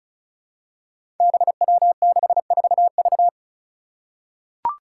Morse code, heard as a keyed tone, about 36 wpm, 700 Hz. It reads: BW64V
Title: Five Challenging Letter-Number Combinations 35wpm